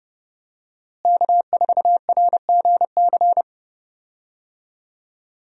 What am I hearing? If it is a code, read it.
Morse code, heard as a keyed tone, about 30 wpm, 700 Hz. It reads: K4RGC
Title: USA Call Signs - 5 Characters - Repeated 2x - 30wpm